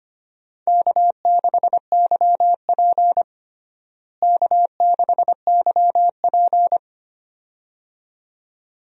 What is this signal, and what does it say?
Morse code, heard as a keyed tone, about 25 wpm, 700 Hz. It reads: K6YP K6YP
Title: USA Call Signs - 4 Characters - Repeated 2x - 25wpm